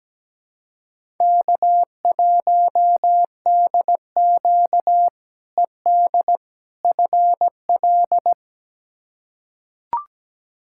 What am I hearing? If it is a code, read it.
Morse code, heard as a keyed tone, about 17 wpm, 700 Hz. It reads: K1DQ ED FL
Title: K1USN SST QSO Practice - 17wpm